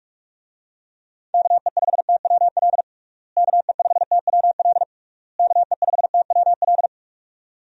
Morse code, heard as a keyed tone, about 47 wpm, 700 Hz. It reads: KE5TWL KE5TWL KE5TWL
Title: USA Call Signs - 6 Characters - Repeated 3x - 45wpm